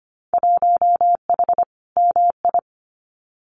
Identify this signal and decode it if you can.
Morse code, heard as a keyed tone, about 25 wpm, 700 Hz. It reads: 1H MI
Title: Winter Field Day Exchanges 25wpm